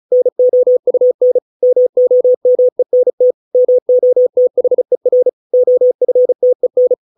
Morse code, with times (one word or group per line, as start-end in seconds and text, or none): 0.1-1.4 NOUN
1.6-3.3 MOMENT
3.5-5.3 MOTHER
5.5-6.9 OFTEN